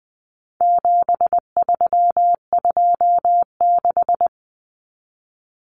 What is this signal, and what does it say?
Morse code, heard as a keyed tone, about 20 wpm, 700 Hz. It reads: 7326
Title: Common Cut Numbers - 4-Digit 20wpm